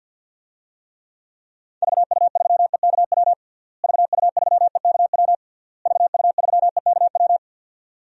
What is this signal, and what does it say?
Morse code, heard as a keyed tone, about 50 wpm, 700 Hz. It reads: VU3EKW VU3EKW VU3EKW
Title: DXCC-100 Call Signs - 6 Characters - Repeated 3x - 50wpm